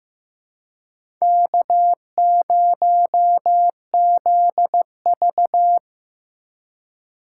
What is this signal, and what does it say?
Morse code, heard as a keyed tone, about 15 wpm, 700 Hz. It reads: K0ZV
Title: USA Call Signs - 4 Characters - 2x Character Spacing - 15wpm